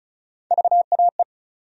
Morse code, heard as a keyed tone, about 34 wpm, 700 Hz. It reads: VAE
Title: USA Call Sign Postfixes - 3 Letters 35wpm